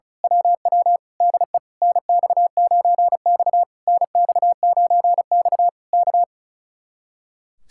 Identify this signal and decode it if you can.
Morse code, heard as a keyed tone, about 32 wpm, 700 Hz. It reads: WW DE NX9X NX9X K
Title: Speed-Racing Common QSO Phrases Encore - 35wpm